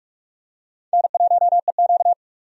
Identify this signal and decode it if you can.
Morse code, heard as a keyed tone, about 43 wpm, 700 Hz. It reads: N1EQ
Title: USA Call Signs - 4 Characters - 45wpm